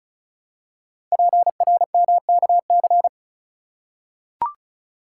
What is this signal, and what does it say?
Morse code, heard as a keyed tone, about 34 wpm, 700 Hz. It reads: PRMKC
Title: Code Groups - 5 Letters - 35wpm